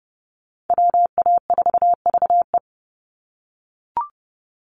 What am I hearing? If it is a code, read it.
Morse code, heard as a keyed tone, about 30 wpm, 700 Hz. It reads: WA4VE
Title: USA Call Signs 30wpm